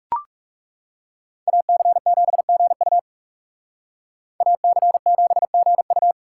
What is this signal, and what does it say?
Morse code, heard as a keyed tone, about 38 wpm, 700 Hz. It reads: AC7GU AC7GU